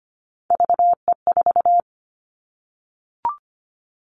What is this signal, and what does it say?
Morse code, heard as a keyed tone, about 25 wpm, 700 Hz. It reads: VE4
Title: Callsign Prefixes for Top 100 DXCC - 25wpm